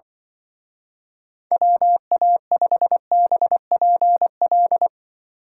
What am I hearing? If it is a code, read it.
Morse code, heard as a keyed tone, about 25 wpm, 700 Hz. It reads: WA5BPL
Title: Speed Racing - USA Call Signs 20wpm